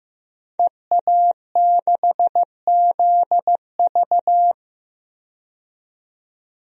Morse code, heard as a keyed tone, about 15 wpm, 700 Hz. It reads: EA6ZV
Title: DXCC-100 Call Signs - 5 Characters - 15wpm